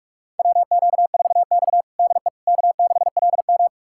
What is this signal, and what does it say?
Morse code, heard as a keyed tone, about 41 wpm, 700 Hz. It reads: WQ4X DE K6LM
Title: USA Callsign DE Callsign - 4 Characters - 45wpm